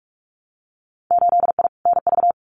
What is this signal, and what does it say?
Morse code, heard as a keyed tone, about 46 wpm, 700 Hz. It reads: 8I NV
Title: Winter Field Day Exchanges Encore 45wpm